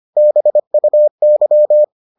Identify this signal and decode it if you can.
Morse code, heard as a keyed tone, about 25 wpm, 600 Hz. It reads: BUY